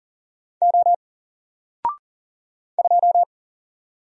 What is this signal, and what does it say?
Morse code, heard as a keyed tone, about 40 wpm, 700 Hz. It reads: O 2
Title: Single Letter-Number 40wpm